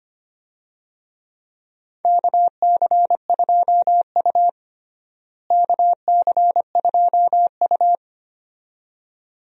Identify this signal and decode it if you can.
Morse code, heard as a keyed tone, about 25 wpm, 700 Hz. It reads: KC2U KC2U
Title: USA Call Signs - 4 Characters - Repeated 2x - 25wpm